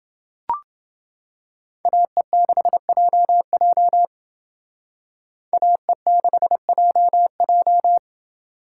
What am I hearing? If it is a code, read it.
Morse code, heard as a keyed tone, about 28 wpm, 700 Hz. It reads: AE6JJ AE6JJ